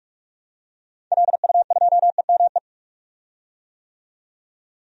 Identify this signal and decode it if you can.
Morse code, heard as a keyed tone, about 41 wpm, 700 Hz. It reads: LU2EME
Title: DXCC-100 Call Signs - 6 Characters - 45wpm